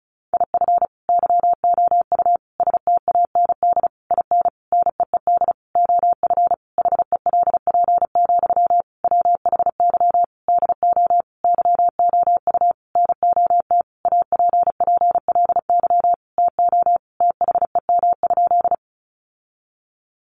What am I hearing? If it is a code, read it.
Morse code, heard as a keyed tone, about 34 wpm, 700 Hz. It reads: IF YOU STAND IN NEED OF HELP, WHY DO YOU NOT APPLY TO THEM?
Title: Sentences from Top 1200 Words 35wpm